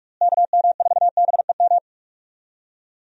Morse code, heard as a keyed tone, about 48 wpm, 700 Hz. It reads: KM4BEM
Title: USA Call Signs - 6 Characters - Repeated 2x - 45wpm